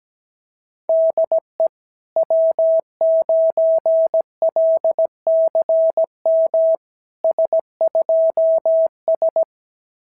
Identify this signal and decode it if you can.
Morse code, heard as a keyed tone, about 17 wpm, 650 Hz. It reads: DE W9LCM S2S
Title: SOTA QSO Practice 17wpm